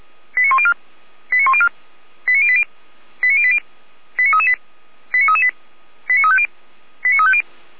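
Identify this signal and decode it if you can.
Five-tone selective call (ZVEI), heard as quick runs of five tones, tones of about 70 ms. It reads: 89104 89104 89080 89080 89338 89338 89350 89350
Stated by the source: ZVEI five-tone call received on SDR.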